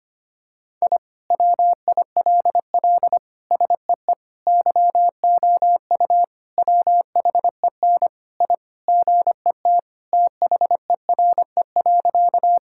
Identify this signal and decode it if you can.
Morse code, heard as a keyed tone, about 25 wpm, 700 Hz. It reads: I WILL SEE YOU WHEN I GET THERE.
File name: practice / 25wpm / Sentences with 2-Word Phrases - 25wpm